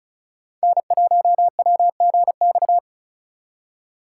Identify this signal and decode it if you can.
Morse code, heard as a keyed tone, about 44 wpm, 700 Hz. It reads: N1WGX